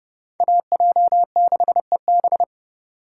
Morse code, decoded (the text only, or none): AJ6EB